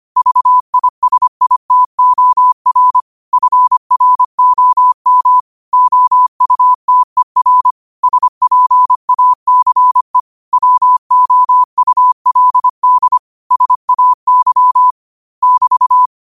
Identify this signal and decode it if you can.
Morse code, heard as a keyed tone, about 25 wpm, 1000 Hz. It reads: VISITOR FROM OUTER SPACE WOULD SAY <BT>